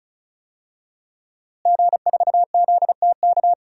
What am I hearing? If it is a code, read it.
Morse code, heard as a keyed tone, about 34 wpm, 700 Hz. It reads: G4ZTK
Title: DXCC-100 Call Signs - 5 Characters - Repeated 2x - 35wpm